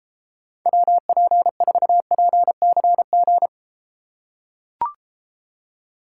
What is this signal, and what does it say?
Morse code, heard as a keyed tone, about 33 wpm, 700 Hz. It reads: WP4PCG